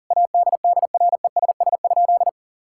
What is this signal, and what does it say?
Morse code, heard as a keyed tone, about 40 wpm, 700 Hz. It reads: ADDRESS?
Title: Sentences from Top 1000 Words - Review Part 6 - 40wpm